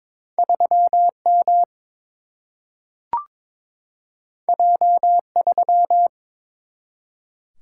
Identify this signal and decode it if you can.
Morse code, heard as a keyed tone, about 21 wpm, 700 Hz. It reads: 3 M J 3
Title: Single-Letter Single-Number 22wpm